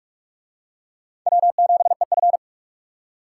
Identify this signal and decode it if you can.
Morse code, heard as a keyed tone, about 38 wpm, 700 Hz. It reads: W7EF